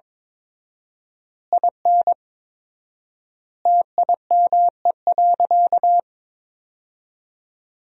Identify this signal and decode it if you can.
Morse code, heard as a keyed tone, about 21 wpm, 700 Hz. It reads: IN TIME.
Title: 4x Word-Spacing - Sentences from Top 300 Words - 22wpm